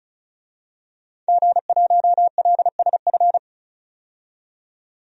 Morse code, heard as a keyed tone, about 34 wpm, 700 Hz. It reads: G1LSF
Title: DXCC-100 Call Signs - 5 Characters - 35wpm